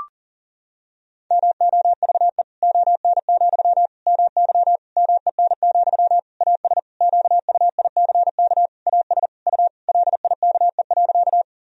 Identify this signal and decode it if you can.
Morse code, heard as a keyed tone, about 40 wpm, 700 Hz. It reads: MOVE ON, MY MEN, AS QUICK AS U LIKE.
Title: Sentences from Top 500 Words and Abbreviations 40wpm